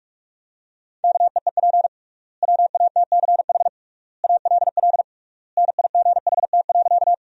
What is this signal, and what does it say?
Morse code, heard as a keyed tone, about 44 wpm, 700 Hz. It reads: KEEP WATCH ALL NIGHT.
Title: 3x Word-Spacing - Sentences from Top 300 Words - 45wpm